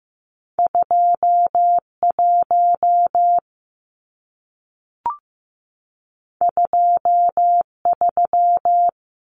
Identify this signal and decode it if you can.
Morse code, heard as a keyed tone, about 15 wpm, 700 Hz. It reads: 21 23
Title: Numbers - 2-Digit 15wpm